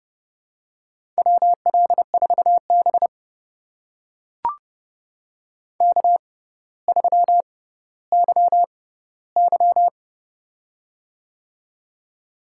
Morse code, heard as a keyed tone, about 30 wpm, spaced as for 15 wpm, 700 Hz. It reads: WL4B K3YY
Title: USA Call Signs - 4 Characters - 6x Character Spacing - 30wpm